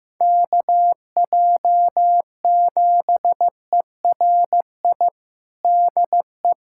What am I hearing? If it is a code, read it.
Morse code, heard as a keyed tone, about 15 wpm, 700 Hz. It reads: KJ7ERI DE
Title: USA Callsign DE Callsign 15wpm